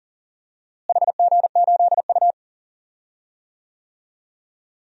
Morse code, heard as a keyed tone, about 40 wpm, 700 Hz. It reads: HG8U
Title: DXCC-100 Call Signs - 4 Characters - Repeated 3x - 40wpm